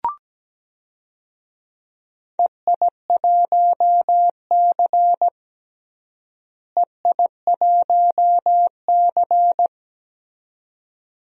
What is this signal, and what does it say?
Morse code, heard as a keyed tone, about 17 wpm, 700 Hz. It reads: EI1C EI1C
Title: DXCC-100 Call Signs - 4 Characters - Repeated 2x - 17wpm